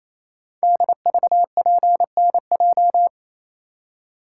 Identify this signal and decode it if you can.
Morse code, heard as a keyed tone, about 28 wpm, 700 Hz. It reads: DVPNJ
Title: Code Groups - 5 Letters - 28wpm